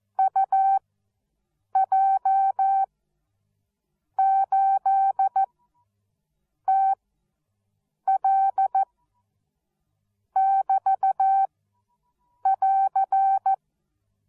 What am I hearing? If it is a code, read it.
Morse code, heard as a keyed tone, about 14 wpm, spaced as for 6 wpm, 780 Hz. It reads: UJ8TL<BT><AR>
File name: Onlinelehrgang / morsen / DJ4UF Morsen13